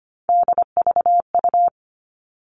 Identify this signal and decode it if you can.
Morse code, heard as a keyed tone, about 25 wpm, 700 Hz. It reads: DVU
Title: All Cut Numbers - 3-Digit 25wpm